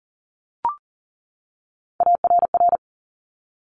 Morse code, heard as a keyed tone, about 39 wpm, 690 Hz. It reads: ARR
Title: Top Three Letter Combinations in Words 40wpm